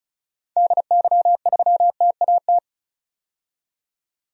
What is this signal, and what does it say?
Morse code, heard as a keyed tone, about 36 wpm, 700 Hz. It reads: DY3TAT
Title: DXCC-100 Call Signs - 6 Characters - 35wpm